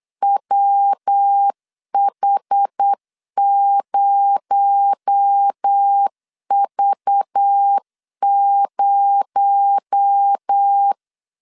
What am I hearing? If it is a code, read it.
Morse code, heard as a keyed tone, about 8 wpm, 800 Hz. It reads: WH0V0
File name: CW / DET / 1 6 1500C S B 8wpm 5142